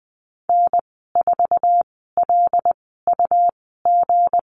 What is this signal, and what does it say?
Morse code, heard as a keyed tone, about 20 wpm, spaced as for 15 wpm, 700 Hz. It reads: N4LUG
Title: 173 Lesson 28g - Number 9 - Call Signs 20wpm